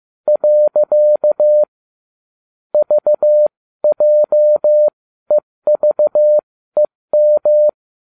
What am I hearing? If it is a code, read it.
Morse code, heard as a keyed tone, about 15 wpm, spaced as for 13 wpm, 600 Hz. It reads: . VJEVEM